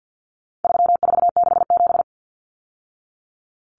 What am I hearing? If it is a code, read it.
Morse code, heard as a keyed tone, about 50 wpm, 700 Hz. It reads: 3467
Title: Numbers - 4-Digit 50wpm